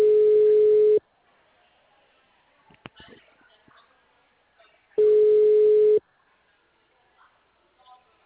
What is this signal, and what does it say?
Ringback tone (Europe, 425 Hz), with speech.